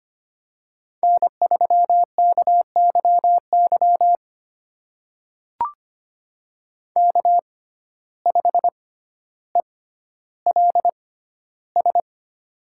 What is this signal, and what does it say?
Morse code, heard as a keyed tone, about 25 wpm, spaced as for 13 wpm, 690 Hz. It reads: N3KYY K5ELS
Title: USA Call Signs - 5 Characters - 6x Character Spacing - 25wpm